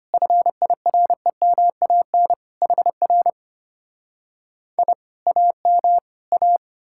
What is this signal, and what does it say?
Morse code, heard as a keyed tone, about 33 wpm, 700 Hz. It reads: FIREMAN HR I AM A